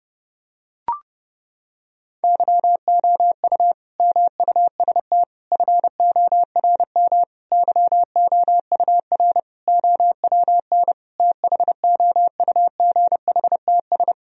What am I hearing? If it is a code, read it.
Morse code, heard as a keyed tone, about 32 wpm, 700 Hz. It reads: YOU MUST FORM YOUR OWN THOUGHTS
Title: Sentences from Top 1000 Words - Review Part 1 - 30wpm